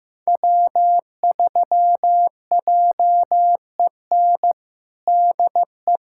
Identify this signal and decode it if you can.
Morse code, heard as a keyed tone, about 16 wpm, 700 Hz. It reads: W3JEN DE
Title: USA Callsign DE Callsign - 5 Characters - 15wpm